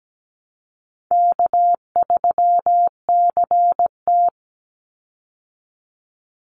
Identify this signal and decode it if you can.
Morse code, heard as a keyed tone, about 17 wpm, 700 Hz. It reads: K3CT
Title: Call Signs Repeated 2x - 17wpm